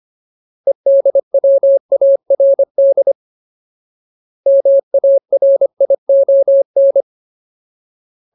Morse code, heard as a keyed tone, about 25 wpm, 550 Hz. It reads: EDWARD MARION
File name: Ultimate Word List / 25WPM Ultimate